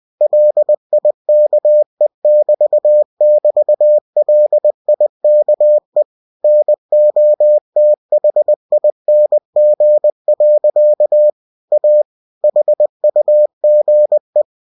Morse code, heard as a keyed tone, about 20 wpm, 600 Hz. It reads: LIKE<BT><BT>LIKE NOTHING. A HUGE
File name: cw-books / winnie-the-pooh / winnie-20wpm-page-0043